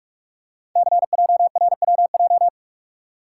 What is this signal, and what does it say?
Morse code, heard as a keyed tone, about 44 wpm, 700 Hz. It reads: CJRWJ
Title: Code Groups - 5 Letters - 45wpm